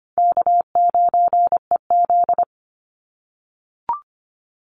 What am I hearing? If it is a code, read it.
Morse code, heard as a keyed tone, about 25 wpm, 700 Hz. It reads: K9EZ
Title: NA SOTA Chaser Callsigns 25wpm